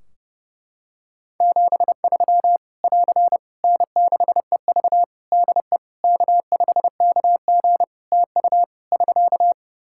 Morse code, heard as a keyed tone, about 30 wpm, 700 Hz. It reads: 73 <AR> N6EV DE K5KG TU <SK>
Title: K1USN SST QSO Practice 30wpm